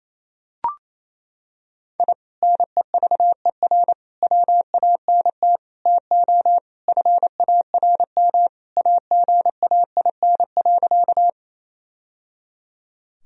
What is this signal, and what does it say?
Morse code, heard as a keyed tone, about 27 wpm, 700 Hz. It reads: I NEVER WANT TO FARM AGAIN.
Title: Sentences from Top 1000 Words - Review Part 2 - 28wpm